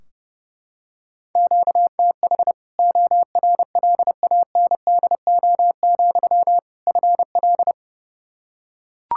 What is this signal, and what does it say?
Morse code, heard as a keyed tone, about 30 wpm, 700 Hz. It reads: QTH ORLANDO, FL
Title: Practice QSO - Exchange 1 - Beginning - 30wpm